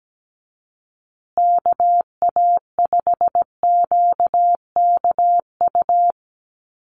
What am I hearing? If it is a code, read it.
Morse code, heard as a keyed tone, about 17 wpm, 700 Hz. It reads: KA5QKU
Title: USA Call Signs Repeated 2x 17wpm